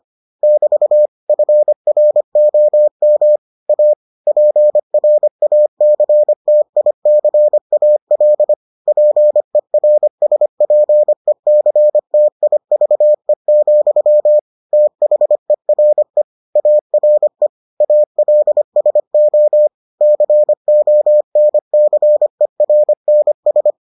Morse code, heard as a keyed tone, about 25 wpm, 600 Hz. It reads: <BT> FROM A PRACTICAL PERSPECTIVE, THERE ARE ALSO CONCERNS